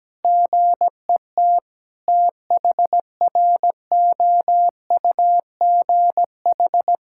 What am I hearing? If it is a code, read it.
Morse code, heard as a keyed tone, about 17 wpm, 700 Hz. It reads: GET THROUGH
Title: Sentences from Top 1000 Words - Review Part 5 - 17wpm